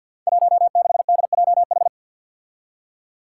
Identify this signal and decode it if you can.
Morse code, heard as a keyed tone, about 52 wpm, 700 Hz. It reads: 16DJH
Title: Five Challenging Letter-Number Combinations 50wpm